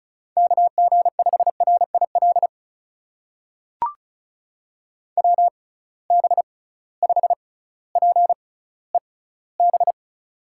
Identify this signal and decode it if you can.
Morse code, heard as a keyed tone, about 36 wpm, spaced as for 18 wpm, 700 Hz. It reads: KG5RIL WB5PEB
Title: USA Call Signs - 6 Characters - 6x Character Spacing - 35wpm